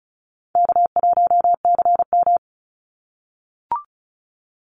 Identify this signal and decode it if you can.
Morse code, heard as a keyed tone, about 34 wpm, 700 Hz. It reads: K1CM
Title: NA SOTA Chaser Callsigns 35wpm